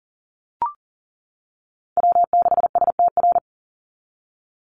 Morse code, heard as a keyed tone, about 40 wpm, 700 Hz. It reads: W6STR
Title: NA SOTA Chaser Callsigns 40wpm